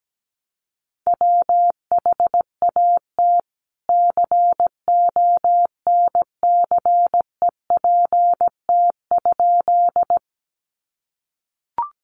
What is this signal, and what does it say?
Morse code, heard as a keyed tone, about 17 wpm, 700 Hz. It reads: WHAT CONCEPT?
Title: Sentences from Top 1100 Words 17wpm